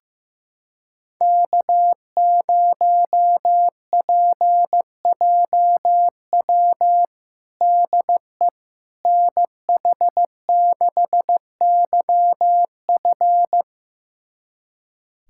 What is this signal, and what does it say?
Morse code, heard as a keyed tone, about 15 wpm, 700 Hz. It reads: K0PJW DE NH6YF
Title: USA Callsign DE Callsign - 5 Characters - 15wpm